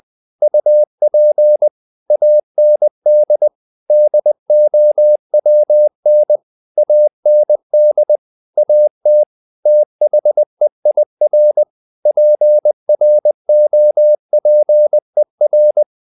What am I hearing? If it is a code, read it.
Morse code, heard as a keyed tone, about 20 wpm, 600 Hz. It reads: UP AND DOWN AND AT THEIR PROPER
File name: the old man and the sea / the old man and the sea part 1 20wpm